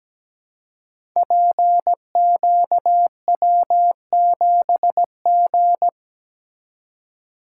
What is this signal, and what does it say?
Morse code, heard as a keyed tone, about 17 wpm, 700 Hz. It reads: PQW7G
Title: Code Groups - 5 Letter:Numbers - 17wpm